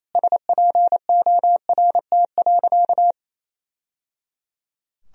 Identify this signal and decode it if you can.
Morse code, heard as a keyed tone, about 27 wpm, 700 Hz. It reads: SPORT.
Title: Sentences from Top 1000 Words - Review Part 4 - 28wpm